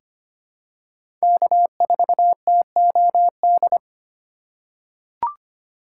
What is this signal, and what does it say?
Morse code, heard as a keyed tone, about 25 wpm, 700 Hz. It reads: K4TOD
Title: USA Call Signs - 5 Characters - 6x Character Spacing - 25wpm